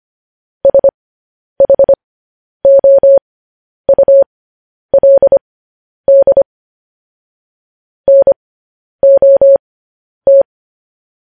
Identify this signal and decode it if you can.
Morse code, heard as a keyed tone, about 25 wpm, spaced as for 10 wpm, 550 Hz. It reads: SHOULD NOT
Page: news headlines in morse code at 10 wpm